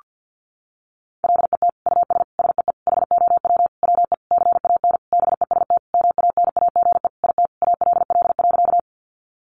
Morse code, heard as a keyed tone, about 50 wpm, 700 Hz. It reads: LET US SEE HOW WE CAN BEST MANAGE IT ALL.
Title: Sentences from Top 1400 Words 50wpm